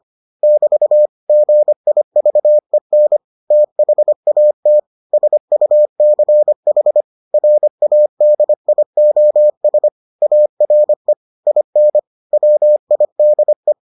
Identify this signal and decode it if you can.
Morse code, heard as a keyed tone, about 25 wpm, 600 Hz. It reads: <BT> GIVEN THAT SUCH RADIOS ARE IN WIDE